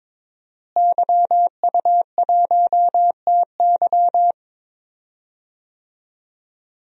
Morse code, heard as a keyed tone, about 23 wpm, 700 Hz. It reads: YU1TY
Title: DXCC-100 Call Signs - 5 Characters - Repeated 2x - 22wpm